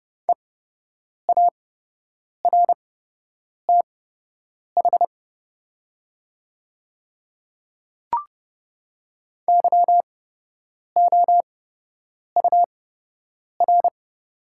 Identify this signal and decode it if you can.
Morse code, heard as a keyed tone, about 30 wpm, spaced as for 8 wpm, 700 Hz. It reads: EARTH YOUR
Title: 8x Character Spacing - Top 200 Words - 30wpm